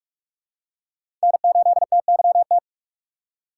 Morse code, heard as a keyed tone, about 46 wpm, 700 Hz. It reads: N8TYT
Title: USA Call Signs - 5 Characters - 6x Character Spacing - 45wpm